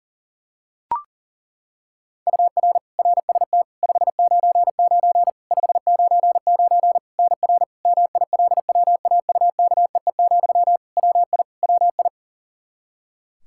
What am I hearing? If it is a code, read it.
Morse code, heard as a keyed tone, about 40 wpm, 700 Hz. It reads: UR RST 599 599 NR MILWAUKEE, WI WI
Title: Practice QSO - Exchange 1 - Beginning - 40wpm